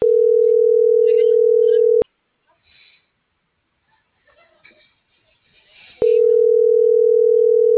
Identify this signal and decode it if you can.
Ringback tone (North America), 440 + 480 Hz.